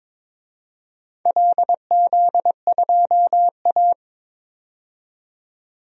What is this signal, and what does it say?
Morse code, heard as a keyed tone, about 22 wpm, 700 Hz. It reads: LZ2A
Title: DXCC-100 Call Signs - 4 Characters - 22wpm